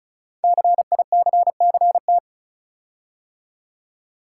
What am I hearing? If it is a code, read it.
Morse code, heard as a keyed tone, about 36 wpm, 690 Hz. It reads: CICCT
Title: Code Groups - 5 Letters - 35wpm